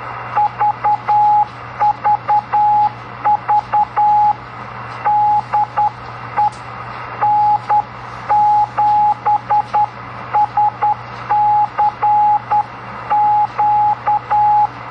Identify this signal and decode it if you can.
Morse code, heard as a keyed tone, about 10 wpm, 840 Hz, with radio noise. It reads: VVV DE N7SCQ
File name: CW Learning Programs / sounds and personalities of cw and transmitters / 06 - Key Clicks